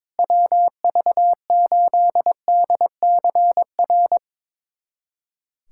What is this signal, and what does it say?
Morse code, heard as a keyed tone, about 22 wpm, 700 Hz. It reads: WV8DCR
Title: USA Callsign DE Callsign 22wpm